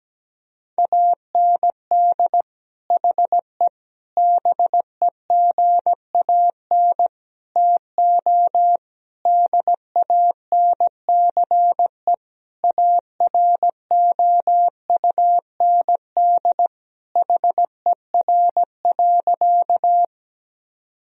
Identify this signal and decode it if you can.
Morse code, heard as a keyed tone, about 17 wpm, 700 Hz. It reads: AND HE BEGAN TO DANCE AROUND HER.
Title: Sentences from Top 1000 Words - Review Part 4 - 17wpm